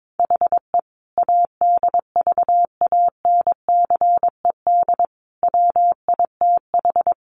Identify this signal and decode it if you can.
Morse code, heard as a keyed tone, about 22 wpm, 700 Hz. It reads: HE ADVANCED WITH
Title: Sentences from Top 1400 Words 22wpm